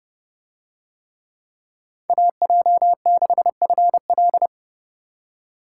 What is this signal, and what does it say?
Morse code, heard as a keyed tone, about 29 wpm, 700 Hz. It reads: AJ6FL